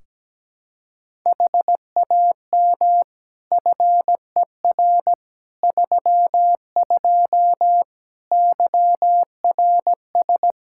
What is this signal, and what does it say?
Morse code, heard as a keyed tone, about 17 wpm, 700 Hz. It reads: HAM FER 32 YRS
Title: Practice QSO - Exchange 3 - More Info - 17wpm